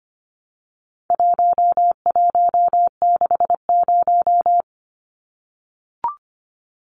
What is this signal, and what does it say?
Morse code, heard as a keyed tone, about 25 wpm, 700 Hz. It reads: 1160
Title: Numbers - 4-Digit 25wpm